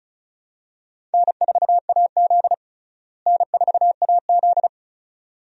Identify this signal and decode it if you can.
Morse code, heard as a keyed tone, about 36 wpm, 700 Hz. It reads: N4AZ N4AZ
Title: USA Call Signs - 4 Characters - Repeated 2x - 35wpm